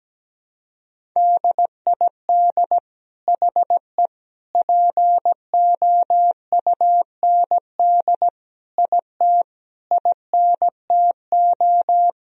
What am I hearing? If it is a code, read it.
Morse code, heard as a keyed tone, about 17 wpm, 700 Hz. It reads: DID HE POUND IT INTO
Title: Sentences from Top 1000 Words - Review Part 2 - 17wpm